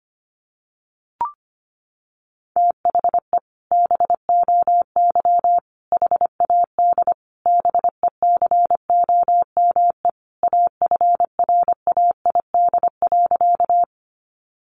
Morse code, heard as a keyed tone, about 25 wpm, 700 Hz. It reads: THE BOY HAD BECOME AFRAID.
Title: Sentences from Top 1500 Words 25wpm